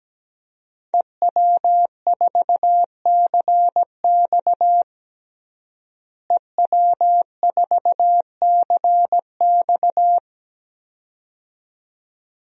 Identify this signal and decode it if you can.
Morse code, heard as a keyed tone, about 17 wpm, 700 Hz. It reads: EW4CX EW4CX
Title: DXCC-100 Call Signs - 5 Characters - Repeated 2x - 17wpm